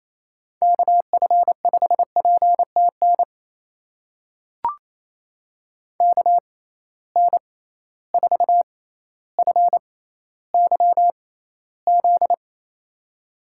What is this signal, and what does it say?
Morse code, heard as a keyed tone, about 28 wpm, spaced as for 14 wpm, 700 Hz. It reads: KF5PTN KN4FYZ
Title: USA Call Signs - 6 Characters - 6x Character Spacing - 28wpm